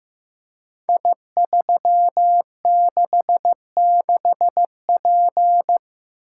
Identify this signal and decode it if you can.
Morse code, heard as a keyed tone, about 15 wpm, 700 Hz. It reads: I366P
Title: Code Groups - 5 Letter:Numbers - 15wpm